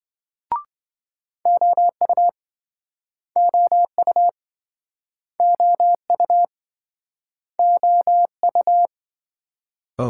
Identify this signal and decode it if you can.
Morse code, heard as a keyed tone, about 26 wpm, 700 Hz. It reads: OU OU OU OU